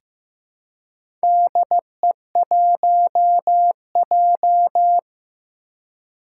Morse code, heard as a keyed tone, about 15 wpm, 700 Hz. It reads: DE1J
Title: Four Challenging Letter-Number Combinations 15wpm